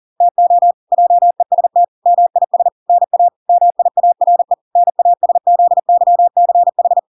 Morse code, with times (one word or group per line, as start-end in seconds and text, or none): 0.2-0.7 TO
0.9-1.8 JEST
2.1-2.7 MIS
2.9-3.3 NA
3.5-4.5 MIARE
4.8-7.0 NASZYCH